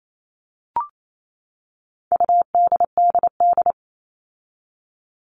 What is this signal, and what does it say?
Morse code, heard as a keyed tone, about 28 wpm, 700 Hz. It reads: UDDD
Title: All Cut Numbers - 4-Digit 28wpm